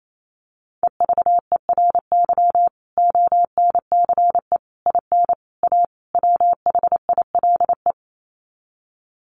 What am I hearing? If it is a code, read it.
Morse code, heard as a keyed tone, about 28 wpm, 700 Hz. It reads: EVERY ONCE IN A WHILE
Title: Sets of 5 Words 28wpm